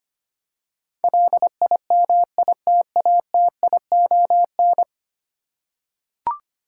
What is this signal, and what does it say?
Morse code, heard as a keyed tone, about 25 wpm, 690 Hz. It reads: LIMITATION
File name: 10 Letter Words - First 3 Letters Given - 25wpm